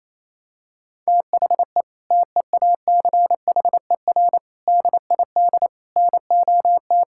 Morse code, heard as a keyed tone, about 27 wpm, 700 Hz. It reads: THE TEACHER DID NOT
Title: Sentences from Top 1000 Words - Review Part 6 - 28wpm